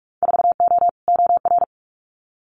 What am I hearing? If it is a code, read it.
Morse code, heard as a keyed tone, about 46 wpm, 700 Hz. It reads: 4O OR
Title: Winter Field Day Exchanges Encore 45wpm